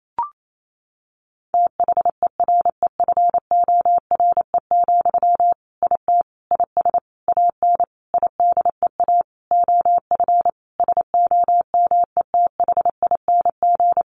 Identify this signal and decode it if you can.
Morse code, heard as a keyed tone, about 28 wpm, 700 Hz. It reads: THEREFORE, IT IS AN IDEA OF SOMETHING
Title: Sentences from Top 1500 Words 28wpm